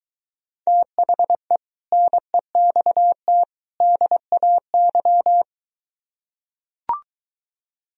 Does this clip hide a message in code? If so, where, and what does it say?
Yes: Morse code from 0.7 to 5.4 s, heard as a keyed tone, about 23 wpm, 700 Hz. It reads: THE NEXT DAY